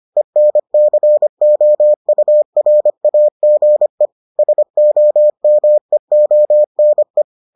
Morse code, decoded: ENCOURAGE SOMEONE

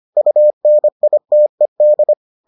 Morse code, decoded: UNITED

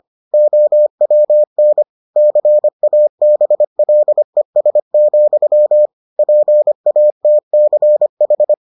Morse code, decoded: OWN CABLES, PATCH

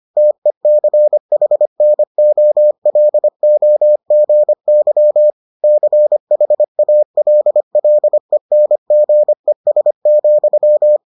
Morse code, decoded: TECHNOLOGY CHALLENGES,

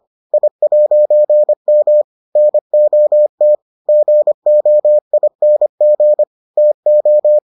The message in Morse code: I'M NOT GOING TO

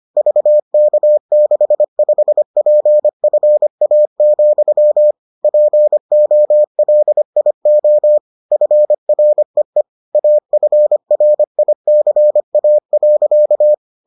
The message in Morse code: VK65PFA, POLIO FREE AFRICA.

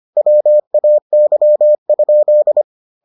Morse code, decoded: WAY?

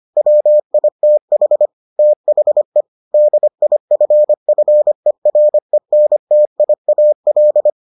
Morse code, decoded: WITH THE DIFFERENTIAL

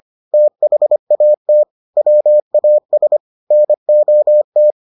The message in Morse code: THAT WAS NOT

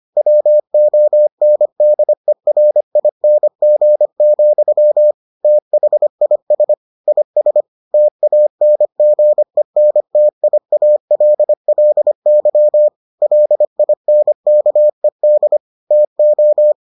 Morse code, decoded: WONDERING, THIS IS TANGENTIALLY LINKED TO

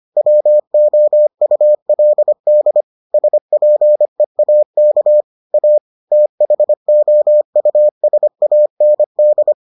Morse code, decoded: WOULD SPEAK A THOUSAND

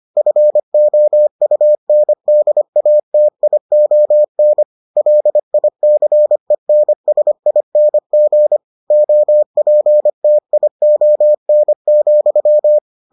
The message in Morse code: FOUNDATION LICENSING OPTION,

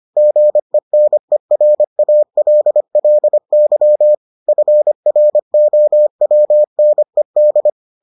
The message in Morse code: GENERALLY FROWNED

